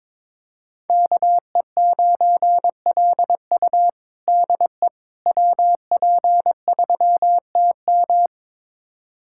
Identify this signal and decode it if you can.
Morse code, heard as a keyed tone, about 22 wpm, 700 Hz. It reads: KE9LU DE WP3TM
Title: USA Callsign DE Callsign - 5 Characters - 22wpm